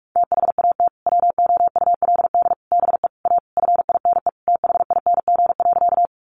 Morse code, heard as a keyed tone, about 44 wpm, 700 Hz. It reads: THAT WOULD BE A FINE THING.
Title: Sentences from Top 500 Words 45wpm